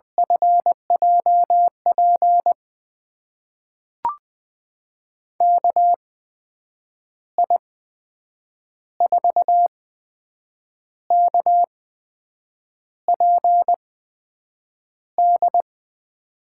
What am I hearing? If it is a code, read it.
Morse code, heard as a keyed tone, about 21 wpm, spaced as for 7 wpm, 700 Hz. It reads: FJP KI4KPD
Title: USA Call Signs - 6 Characters - 8x Character Spacing - 20wpm